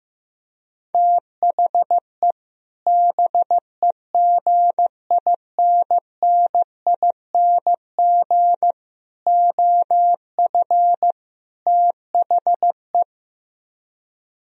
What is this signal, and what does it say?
Morse code, heard as a keyed tone, about 15 wpm, 700 Hz. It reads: THE BEGINNING OF THE
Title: Speed-Racing - Sets of 4 Words 15wpm